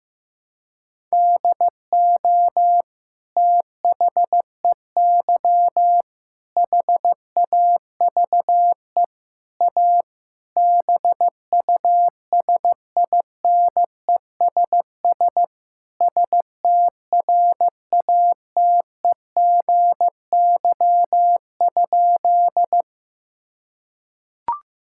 Morse code, heard as a keyed tone, about 15 wpm, 700 Hz. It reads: DO THEY HAVE A BUSINESS STRATEGY?
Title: Sentences from Top 1000 Words - Review Part 5 - 15wpm